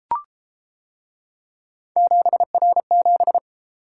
Morse code, heard as a keyed tone, about 32 wpm, 700 Hz. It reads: 7R7